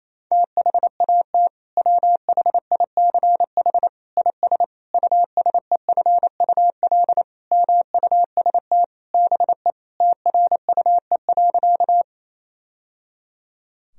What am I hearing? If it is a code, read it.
Morse code, heard as a keyed tone, about 28 wpm, 700 Hz. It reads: THAT WHICH IS USEFUL MUST BE TRUE.
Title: Sentences from Top 1000 Words - Review Part 4 - 28wpm